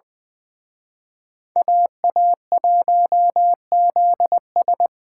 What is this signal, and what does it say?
Morse code, heard as a keyed tone, about 23 wpm, 700 Hz. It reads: AA1ZS
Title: Speed Racing - USA 2x2 Call Signs 20wpm